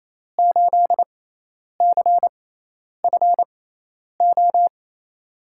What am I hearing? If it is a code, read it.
Morse code, heard as a keyed tone, about 28 wpm, spaced as for 10 wpm, 700 Hz. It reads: 8CFO
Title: USA Call Signs - 5 Characters - 6x Character Spacing - 28wpm